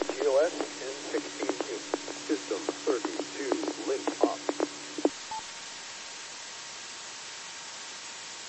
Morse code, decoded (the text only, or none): none